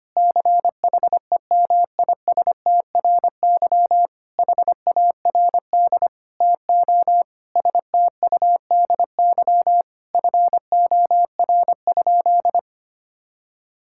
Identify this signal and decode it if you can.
Morse code, heard as a keyed tone, about 25 wpm, 700 Hz. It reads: CHEMISTRY HARD TO STUDY FOR?
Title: Sentences from Top 1000 Words - Review Part 6 - 25wpm